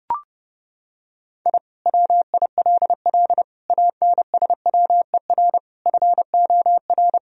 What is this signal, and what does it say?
Morse code, heard as a keyed tone, about 32 wpm, 700 Hz. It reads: I WILL ANSWER FOR
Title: Sentences from Top 1000 Words - Review Part 4 - 30wpm